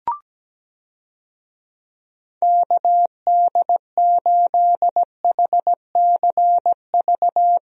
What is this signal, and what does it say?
Morse code, heard as a keyed tone, about 17 wpm, 700 Hz. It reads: KD8HCV
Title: USA Call Signs - 6 Characters - Repeated 2x - 17wpm